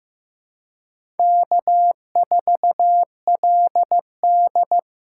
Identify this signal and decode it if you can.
Morse code, heard as a keyed tone, about 15 wpm, 700 Hz. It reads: K4LD
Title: USA Call Signs - 4x Character Spacing - 15wpm